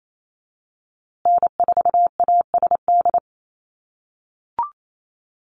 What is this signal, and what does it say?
Morse code, heard as a keyed tone, about 28 wpm, 700 Hz. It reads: N4ASD
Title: USA Call Signs Repeated 2x 28wpm